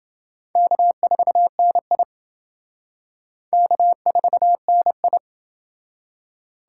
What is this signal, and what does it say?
Morse code, heard as a keyed tone, about 28 wpm, 700 Hz. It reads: K4NI K4NI